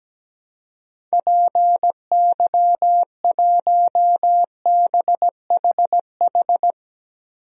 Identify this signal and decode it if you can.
Morse code, heard as a keyed tone, about 17 wpm, 700 Hz. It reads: PY1BHH
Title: DXCC-100 Call Signs - 6 Characters - Repeated 2x - 17wpm